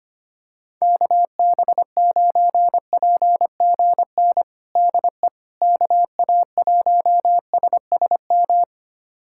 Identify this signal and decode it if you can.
Morse code, heard as a keyed tone, about 25 wpm, 700 Hz. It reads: KB9PGN DE KA1SSM
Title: USA Callsign DE Callsign - 6 Characters - 25wpm